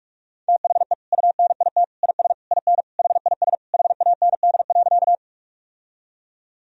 Morse code, heard as a keyed tone, about 46 wpm, 700 Hz. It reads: THE UNIT IS IN HIS HAND.
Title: Sentences from Top 1000 Words - Review Part 2 - 45wpm